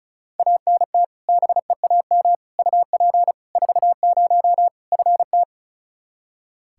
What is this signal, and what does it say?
Morse code, heard as a keyed tone, about 36 wpm, 700 Hz. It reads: ANT BEAM UP 40 FT
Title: Practice QSO - Exchange 2 - Info - 35wpm